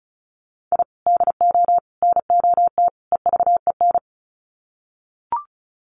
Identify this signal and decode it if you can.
Morse code, heard as a keyed tone, about 34 wpm, 700 Hz. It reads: I DO NOT EVEN
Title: Sets of 4 Words 35wpm